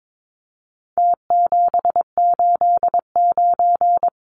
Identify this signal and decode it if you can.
Morse code, heard as a keyed tone, about 22 wpm, 700 Hz. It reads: T789
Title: Numbers - Contest Style - 4-Digit 22wpm